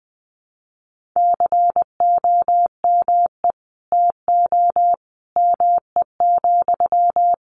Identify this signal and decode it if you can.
Morse code, heard as a keyed tone, about 20 wpm, 700 Hz. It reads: COME TO ME,
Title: Sentences from Top 1500 Words 20wpm